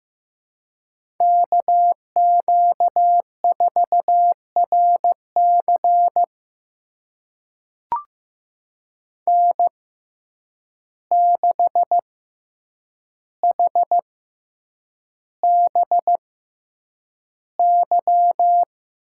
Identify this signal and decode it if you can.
Morse code, heard as a keyed tone, about 15 wpm, spaced as for 8 wpm, 700 Hz. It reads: KQ4RC N6HBY
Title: USA Call Signs - 5 Characters - 6x Character Spacing - 15wpm